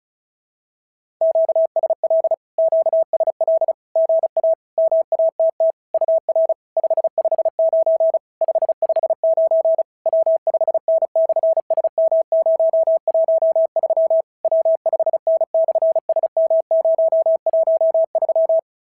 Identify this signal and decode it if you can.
Morse code, heard as a keyed tone, about 33 wpm, 650 Hz. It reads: QSL QSL GA MATT UR 559 559 W5N/SM013 W5N/SM013
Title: SOTA QSO Practice 35wpm